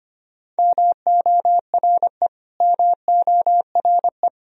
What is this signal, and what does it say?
Morse code, heard as a keyed tone, about 25 wpm, 700 Hz. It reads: MORE MORE
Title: Top 100 Words Repeated 25wpm